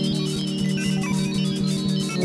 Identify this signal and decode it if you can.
There is no code.